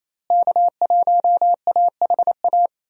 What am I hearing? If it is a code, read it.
Morse code, heard as a keyed tone, about 28 wpm, 700 Hz. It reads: K1AHA
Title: USA Call Signs - 5 Characters - Repeated 3x - 28wpm